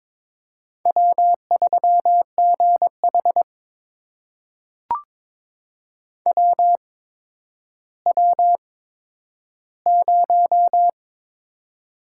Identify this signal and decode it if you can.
Morse code, heard as a keyed tone, about 22 wpm, spaced as for 11 wpm, 700 Hz. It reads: W3GH WW0
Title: USA Call Signs - 4 Characters - 8x Character Spacing - 22wpm